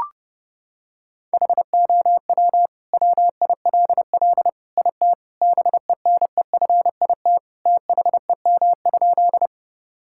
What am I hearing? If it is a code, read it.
Morse code, heard as a keyed tone, about 30 wpm, 700 Hz. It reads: HOW WILL IT BENEFIT THEM?
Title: Sentences from Top 1000 Words - Review Part 6 - 30wpm